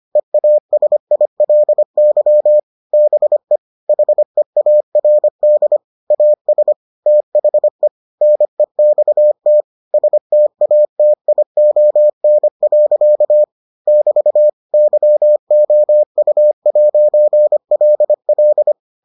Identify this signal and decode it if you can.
Morse code, heard as a keyed tone, about 25 wpm, 600 Hz. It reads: EASILY BE HEARD AS THE NEXT STATION. <BT> YOU'LL